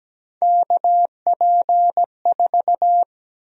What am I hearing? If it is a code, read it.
Morse code, heard as a keyed tone, about 17 wpm, 700 Hz. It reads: KP4
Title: USA Call Sign Prefixes - 17wpm